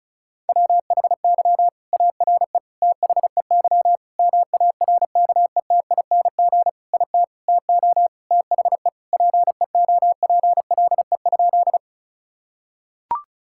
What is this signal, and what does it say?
Morse code, heard as a keyed tone, about 36 wpm, 700 Hz. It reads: WHY ARE THEY MARKETING IT TO THE PEOPLE?
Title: Sentences from Top 1000 Words - Review Part 5 - 35wpm